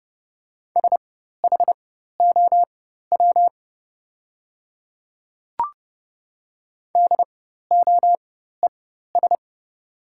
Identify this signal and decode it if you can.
Morse code, heard as a keyed tone, about 30 wpm, spaced as for 14 wpm, 700 Hz. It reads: SHOW DOES
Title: 4x Character Spacing - Top 200 Words - 30wpm